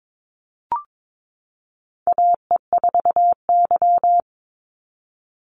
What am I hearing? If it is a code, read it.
Morse code, heard as a keyed tone, about 22 wpm, 700 Hz. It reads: AE4Y
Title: NA SOTA Chaser Callsigns 22wpm